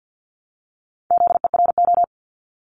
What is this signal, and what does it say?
Morse code, heard as a keyed tone, about 50 wpm, 700 Hz. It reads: ZERO
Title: Multi-letter Phonemes Repeated 3x 50wpm